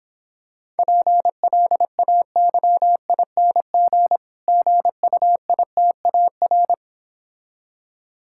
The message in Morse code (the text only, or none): PLAYING GUITAR